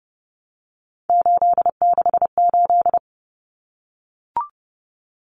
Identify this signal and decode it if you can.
Morse code, heard as a keyed tone, about 30 wpm, 700 Hz. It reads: 868
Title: Numbers - 3-Digit 30wpm